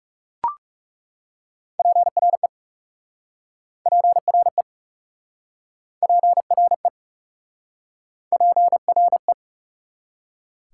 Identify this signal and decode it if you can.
Morse code, heard as a keyed tone, about 40 wpm, 690 Hz. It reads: PRE PRE PRE PRE